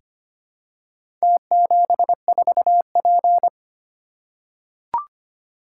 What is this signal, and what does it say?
Morse code, heard as a keyed tone, about 25 wpm, 700 Hz. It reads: T74P
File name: practice / 25wpm / Four Letter-Number Combinations - 25wpm